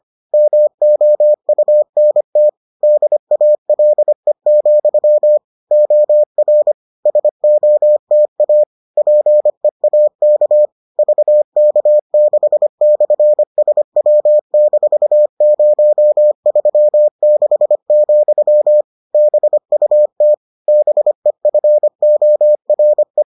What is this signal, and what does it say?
Morse code, heard as a keyed tone, about 25 wpm, 600 Hz. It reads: MOUNT DALE, OR SOTA PEAK VK6/SW-036, BUT BEFORE